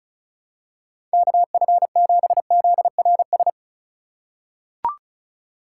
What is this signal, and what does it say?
Morse code, heard as a keyed tone, about 34 wpm, 700 Hz. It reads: KF7ZRS
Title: USA Call Signs - 6 Characters - 2x Character Spacing - 35wpm